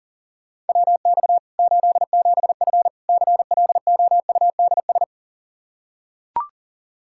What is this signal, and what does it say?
Morse code, heard as a keyed tone, about 40 wpm, 700 Hz. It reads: WX 87F CLOUDS
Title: K1USN SST QSO Practice 40wpm